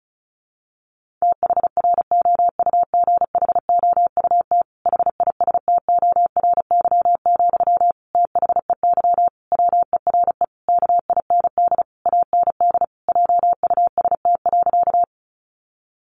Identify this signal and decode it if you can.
Morse code, heard as a keyed tone, about 34 wpm, 700 Hz. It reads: THROUGHOUT HISTORY, THEY WERE KIND AND JUST.
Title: Sentences from Top 800 Words 35wpm